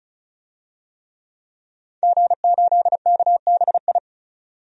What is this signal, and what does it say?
Morse code, heard as a keyed tone, about 34 wpm, 700 Hz. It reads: G8KBI
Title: DXCC-100 Call Signs - 5 Characters - 35wpm